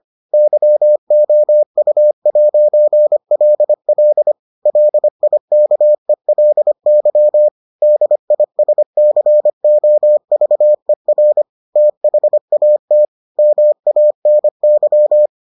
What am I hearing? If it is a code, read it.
Morse code, heard as a keyed tone, about 25 wpm, 600 Hz. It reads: YOU'LL LIKELY DISCOVER THAT MANY